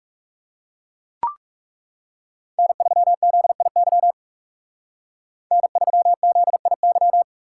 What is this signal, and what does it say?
Morse code, heard as a keyed tone, about 37 wpm, 700 Hz. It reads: N3ZIY N3ZIY
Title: Speed Racing - USA 1x3 Call Signs 30wpm